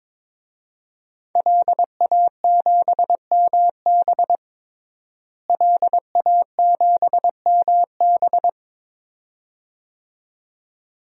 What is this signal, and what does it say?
Morse code, heard as a keyed tone, about 22 wpm, 700 Hz. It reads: LA7MB LA7MB
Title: DXCC-100 Call Signs - 5 Characters - Repeated 2x - 22wpm